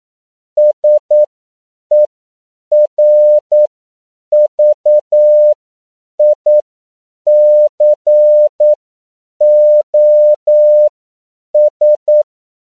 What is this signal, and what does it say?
Morse code, heard as a keyed tone, about 9 wpm, spaced as for 7 wpm, 600 Hz. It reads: SERVICOS
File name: CW / 10PPM